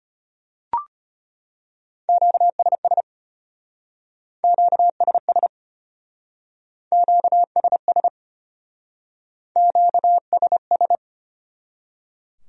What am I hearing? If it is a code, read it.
Morse code, heard as a keyed tone, about 32 wpm, 700 Hz. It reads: QSS QSS QSS QSS